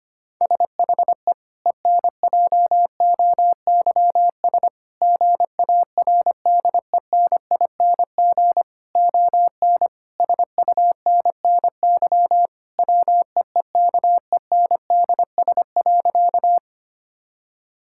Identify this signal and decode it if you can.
Morse code, heard as a keyed tone, about 25 wpm, 700 Hz. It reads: SHE ENJOYS GARDENING ON SUNNY WEEKENDS.
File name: practice / 25wpm / Difficult Sentences with Hints - 25wpm